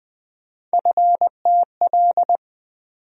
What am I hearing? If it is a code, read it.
Morse code, heard as a keyed tone, about 20 wpm, 700 Hz. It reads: FTL
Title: USA Call Sign Postfixes - 3 Letters 20wpm